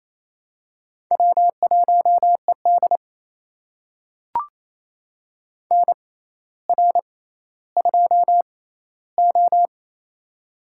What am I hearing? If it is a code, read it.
Morse code, heard as a keyed tone, about 28 wpm, spaced as for 14 wpm, 700 Hz. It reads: W1ED NR2O
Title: USA Call Signs - 6x Character Spacing - 28wpm